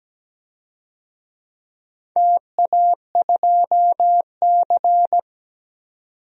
Morse code, heard as a keyed tone, about 17 wpm, 700 Hz. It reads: TA2C
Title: DXCC-100 Call Signs - 4 Characters - 17wpm